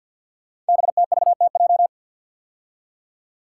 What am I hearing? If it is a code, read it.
Morse code, heard as a keyed tone, about 46 wpm, 700 Hz. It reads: BTVTJ
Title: Code Groups - 5 Letters - 50wpm